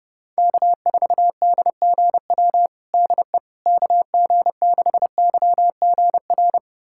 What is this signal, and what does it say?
Morse code, heard as a keyed tone, about 32 wpm, 700 Hz. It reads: K4DGW DE KG6YGR
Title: USA Callsign DE Callsign 30wpm